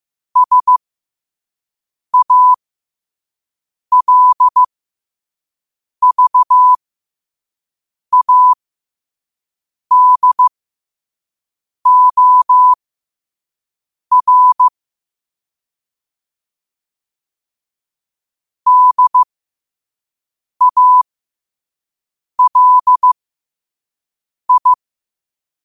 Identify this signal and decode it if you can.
Morse code, heard as a keyed tone, about 15 wpm, spaced as for 5 wpm, 1000 Hz. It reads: SALVADOR DALI
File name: qotdfiles / 5wpm